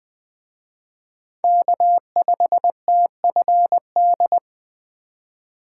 Morse code, heard as a keyed tone, about 20 wpm, 700 Hz. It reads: K5TFD
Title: Speed Racing - USA 1x3 Call Signs 17wpm